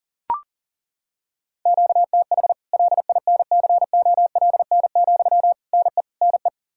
Morse code, heard as a keyed tone, about 40 wpm, 700 Hz. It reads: QTH LINCOLN, NE NE
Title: Practice QSO - Exchange 1 - Beginning - 40wpm